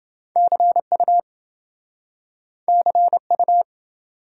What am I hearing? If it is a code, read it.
Morse code, heard as a keyed tone, about 27 wpm, 700 Hz. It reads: CU CU